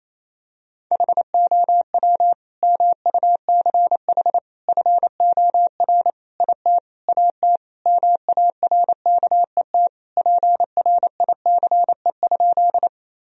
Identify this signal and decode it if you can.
Morse code, heard as a keyed tone, about 28 wpm, 700 Hz. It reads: HOW MUCH FOR IT AT MARKET PRICE?
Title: Sentences from Top 1000 Words - Review Part 3 - 28wpm